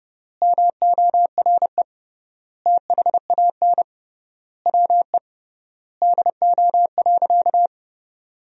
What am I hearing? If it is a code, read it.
Morse code, heard as a keyed tone, about 30 wpm, 700 Hz. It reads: MORE THAN WE DO.
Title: 3x Word-Spacing - Sentences from Top 100 Words - 30wpm